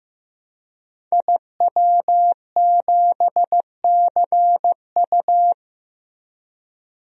Morse code, heard as a keyed tone, about 15 wpm, 700 Hz. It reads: IW7CU
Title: DXCC-100 Call Signs - 5 Characters - Repeated 2x - 15wpm